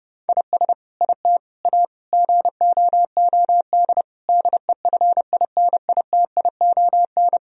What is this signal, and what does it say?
Morse code, heard as a keyed tone, about 30 wpm, 700 Hz. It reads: IS IT A GOOD DEFINITION
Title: Sentences from Top 1000 Words - Review Part 4 - 30wpm